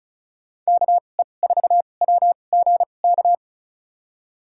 Morse code, heard as a keyed tone, about 36 wpm, spaced as for 25 wpm, 700 Hz. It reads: KE4WGK
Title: USA Call Signs - 6 Characters - 2x Character Spacing - 35wpm